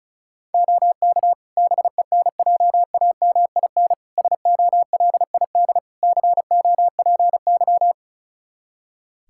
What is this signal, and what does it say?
Morse code, heard as a keyed tone, about 36 wpm, 700 Hz. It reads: OK BENJAMIN SOLID COPY
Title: K1USN SST QSO Practice 35wpm